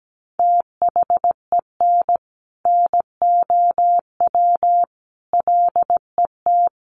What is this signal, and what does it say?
Morse code, heard as a keyed tone, about 17 wpm, 700 Hz. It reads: THEN NOW LET
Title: Sentences from Top 1500 Words 17wpm